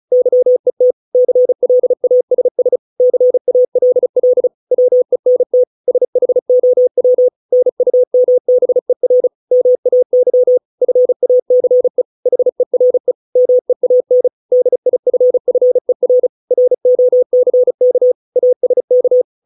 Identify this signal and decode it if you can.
Morse code, heard as a keyed tone, about 35 wpm, 500 Hz. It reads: YET CLASS CALL WENT SHOW NUMBER MAY FACE HERE MEAN DIFFER ROCK ASK